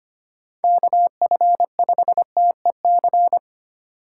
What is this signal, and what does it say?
Morse code, heard as a keyed tone, about 25 wpm, 700 Hz. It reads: KF5TEC
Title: USA Call Signs - 4x Character Spacing - 25wpm